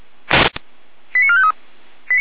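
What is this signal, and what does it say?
Five-tone selective call (ZVEI), heard as quick runs of five tones, tones of about 70 ms. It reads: 89453
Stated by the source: ZVEI received on SDR, from the audio server of the Feuerwehr Orlamünde (fire brigade).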